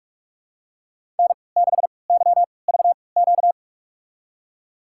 Morse code, heard as a keyed tone, about 46 wpm, spaced as for 26 wpm, 700 Hz. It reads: N6YVQ
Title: USA Call Signs - 5 Characters - 3x Character Spacing - 45wpm